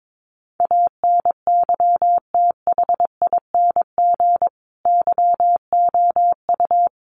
Morse code, heard as a keyed tone, about 22 wpm, 700 Hz. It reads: ANYTHING YOU
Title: Sentences from Top 1000 Words 22wpm